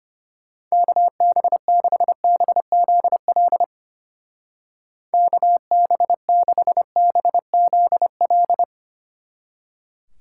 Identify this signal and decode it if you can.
Morse code, heard as a keyed tone, about 33 wpm, 700 Hz. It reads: KB6BZL KB6BZL